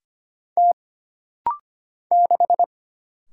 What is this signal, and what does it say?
Morse code, heard as a keyed tone, about 25 wpm, 700 Hz. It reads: T 6
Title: Single Letter-Number Alt - Rapid-Fire 25wpm